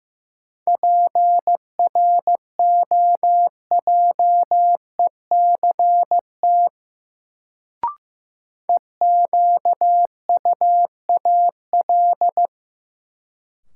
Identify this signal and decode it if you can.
Morse code, heard as a keyed tone, about 15 wpm, 700 Hz. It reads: PROJECT EQUAL
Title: Sentences from Top 1400 Words - New Word Drill - 15wpm